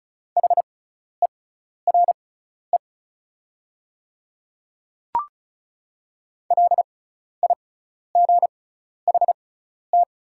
Morse code, heard as a keyed tone, about 34 wpm, spaced as for 12 wpm, 690 Hz. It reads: HERE LIGHT
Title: 6x Character Spacing - Top 300 Words - 35wpm